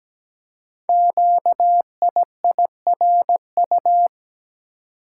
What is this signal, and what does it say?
Morse code, heard as a keyed tone, about 17 wpm, 700 Hz. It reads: QIIRU
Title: Code Groups - 5 Letters - 17wpm